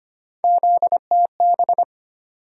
Morse code, heard as a keyed tone, about 25 wpm, 700 Hz. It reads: ZTB
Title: USA Call Sign Postfixes - 3 Letters 25wpm